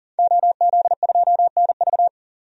Morse code, heard as a keyed tone, about 39 wpm, 700 Hz. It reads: OZ2NV
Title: DXCC-100 Call Signs - 5 Characters - 40wpm